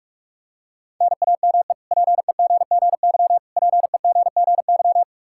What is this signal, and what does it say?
Morse code, heard as a keyed tone, about 46 wpm, 700 Hz. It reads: NAME PEGGY PEGGY
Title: K1USN SST QSO Practice 45wpm